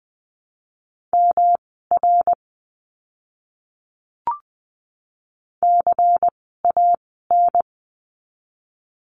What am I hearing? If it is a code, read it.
Morse code, heard as a keyed tone, about 20 wpm, spaced as for 14 wpm, 700 Hz. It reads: MR CAN
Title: 244 Lesson 39f - Letter z - Words 20wpm